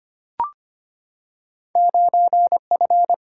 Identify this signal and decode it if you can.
Morse code, heard as a keyed tone, about 25 wpm, 700 Hz. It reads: 9F
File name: Two Letter-Number Combinations - 25wpm